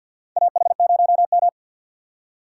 Morse code, heard as a keyed tone, about 50 wpm, 700 Hz. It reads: AH0M
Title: USA Call Signs - 4 Characters - 3x Character Spacing - 50wpm